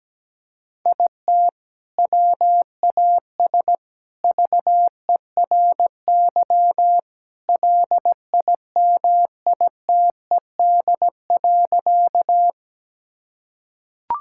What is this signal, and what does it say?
Morse code, heard as a keyed tone, about 17 wpm, 700 Hz. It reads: IT WAS VERY LIMITED.
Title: Sentences from Top 1000 Words - Review Part 6 - 17wpm